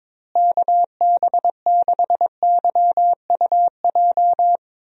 Morse code, heard as a keyed tone, about 22 wpm, 700 Hz. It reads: KB6YUJ
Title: USA Call Signs - 6 Characters - Repeated 2x - 22wpm